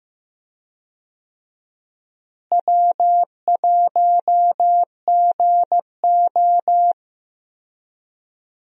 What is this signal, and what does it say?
Morse code, heard as a keyed tone, about 15 wpm, 700 Hz. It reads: W1GO
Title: USA Call Signs - 4 Characters - Repeated 3x - 15wpm